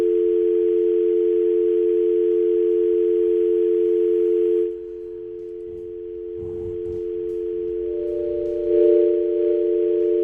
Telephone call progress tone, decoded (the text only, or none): dial tone